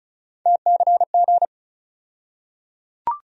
Morse code, heard as a keyed tone, about 34 wpm, 700 Hz. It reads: TCG
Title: USA Call Sign Postfixes - 3 Letters 35wpm